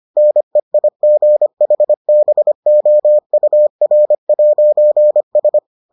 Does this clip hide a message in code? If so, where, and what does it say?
Yes: Morse code from 0.2 to 5.6 s, heard as a keyed tone, about 25 wpm, 600 Hz. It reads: NEIGHBOUR'S